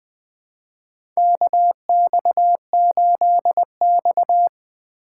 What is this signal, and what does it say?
Morse code, heard as a keyed tone, about 21 wpm, 700 Hz. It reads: KX8X
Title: K1USN SST QSO Practice - 20wpm